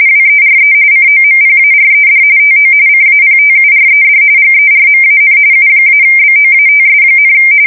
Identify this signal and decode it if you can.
RTTY, about 45.45 baud. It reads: WERE COMPARED TO CAPTURED GERMAN U-BOATS,
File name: fsk170 / usnavy4